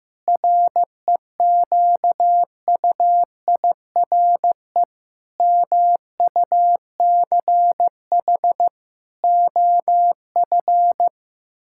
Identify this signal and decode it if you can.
Morse code, heard as a keyed tone, about 15 wpm, 700 Hz. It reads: REQUIRE MUCH OF
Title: Sentences from Top 1000 Words - Review Part 6 - 15wpm